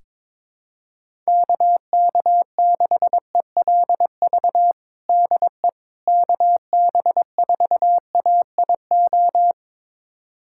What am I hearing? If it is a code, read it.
Morse code, heard as a keyed tone, about 22 wpm, 700 Hz. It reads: KK6ELV DE KB4AIO
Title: USA Callsign DE Callsign - 6 Characters - 22wpm